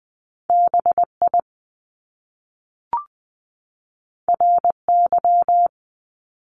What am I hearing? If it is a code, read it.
Morse code, heard as a keyed tone, about 20 wpm, 700 Hz. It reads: B I R Y
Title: Top Two Letter Combinations in Words 20wpm